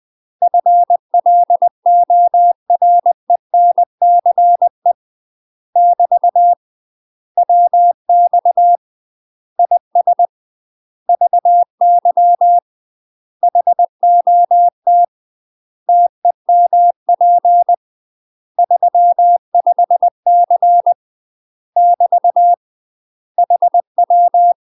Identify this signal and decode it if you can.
Morse code, heard as a keyed tone, about 20 wpm, 700 Hz. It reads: FLORENCE <BT> WX IS VY HOT TEMP 35C <BT> HW
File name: UK MASTER QSOs TRAINING / 20WPM / UK MASTER QSOs 0066